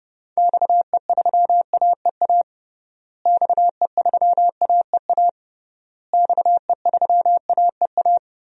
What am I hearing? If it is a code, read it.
Morse code, heard as a keyed tone, about 30 wpm, 700 Hz. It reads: XE3AEA XE3AEA XE3AEA
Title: DXCC-100 Call Signs - 6 Characters - Repeated 3x - 30wpm